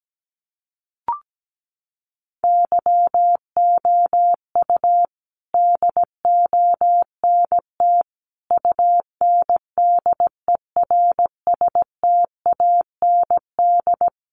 Morse code, heard as a keyed tone, about 17 wpm, 700 Hz. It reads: YOU DONT UNDERSTAND
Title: Sentences from Top 1000 Words 17wpm